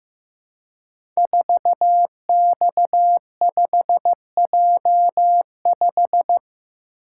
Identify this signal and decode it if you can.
Morse code, heard as a keyed tone, about 15 wpm, 700 Hz. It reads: 4X5J5
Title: Code Groups - 5 Letter:Numbers - 15wpm